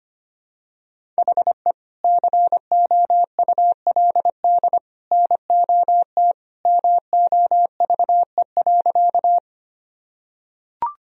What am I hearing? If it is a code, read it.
Morse code, heard as a keyed tone, about 25 wpm, 700 Hz. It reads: HE COULD NOT MOVE.
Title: Sentences from Top 1000 Words - Review Part 1 - 25wpm